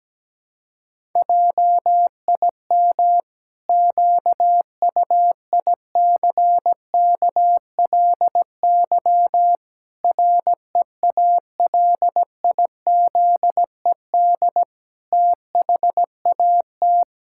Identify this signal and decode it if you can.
Morse code, heard as a keyed tone, about 17 wpm, 700 Hz. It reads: JIM QUICKLY REALIZED THAT
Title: Pangram Challenge v2 - 17wpm